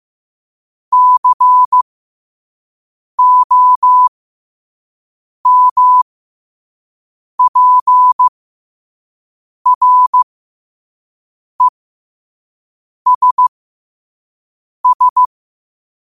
Morse code, heard as a keyed tone, about 15 wpm, spaced as for 5 wpm, 1000 Hz. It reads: COMPRESS
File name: 5wpm